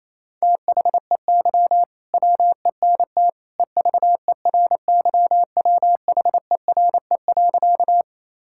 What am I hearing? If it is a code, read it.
Morse code, heard as a keyed tone, about 28 wpm, 700 Hz. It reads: THEY WENT EVERYWHERE.
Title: Sentences from Top 1600 Words 28wpm